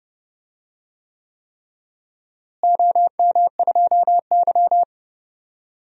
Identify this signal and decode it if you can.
Morse code, heard as a keyed tone, about 30 wpm, 700 Hz. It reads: OM2Y
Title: DXCC-100 Call Signs - 4 Characters - Repeated 3x - 30wpm